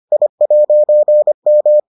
Morse code, heard as a keyed tone, about 25 wpm, 600 Hz. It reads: I'M